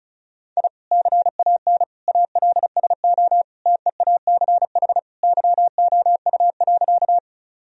Morse code, heard as a keyed tone, about 34 wpm, 700 Hz. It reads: I CAN ALSO TEACH YOU.
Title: Sentences from Top 1000 Words - Review Part 2 - 35wpm